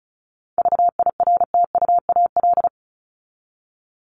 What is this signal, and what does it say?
Morse code, heard as a keyed tone, about 34 wpm, 700 Hz. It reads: VIRTUAL
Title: LY Suffix - Top 100 Words with Base Words 35wpm